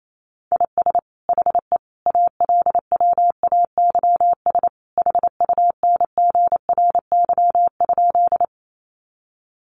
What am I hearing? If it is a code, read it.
Morse code, heard as a keyed tone, about 28 wpm, 700 Hz. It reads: IS HE ALWAYS HUNGRY?
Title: Sentences from Top 1400 Words 28wpm